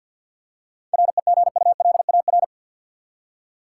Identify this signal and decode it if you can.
Morse code, heard as a keyed tone, about 51 wpm, 700 Hz. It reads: REGULAR
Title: Seven Letter Words - 50wpm